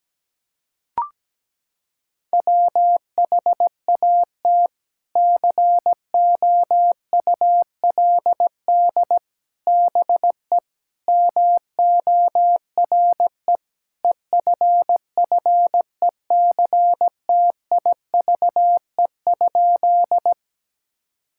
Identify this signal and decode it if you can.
Morse code, heard as a keyed tone, about 17 wpm, 700 Hz. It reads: WHAT COULD BE MORE EFFECTIVE?
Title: Sentences from Top 1000 Words - Review Part 6 - 17wpm